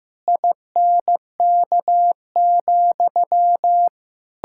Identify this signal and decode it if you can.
Morse code, heard as a keyed tone, about 15 wpm, 700 Hz. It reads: INK,
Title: Pangram Challenge v2 - 15wpm